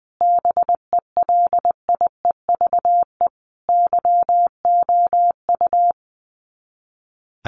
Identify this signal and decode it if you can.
Morse code, heard as a keyed tone, about 20 wpm, 700 Hz. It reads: BELIEVE YOU
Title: Sets of 5 Words Encore 20wpm